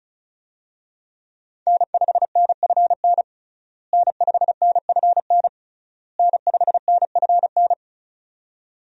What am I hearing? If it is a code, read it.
Morse code, heard as a keyed tone, about 36 wpm, 700 Hz. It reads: N5NFN N5NFN N5NFN
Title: USA Call Signs - 5 Characters - Repeated 3x - 35wpm